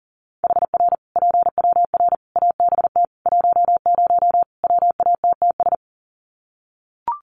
Morse code, heard as a keyed tone, about 40 wpm, 700 Hz. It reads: HR PWR ABT 10 WATTS
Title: Common QSO Phrases 40wpm